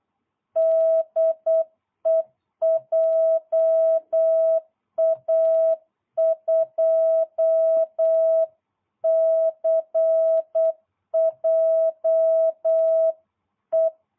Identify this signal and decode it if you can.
Morse code, heard as a keyed tone, about 8 wpm, 660 Hz. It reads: DE JA2CJE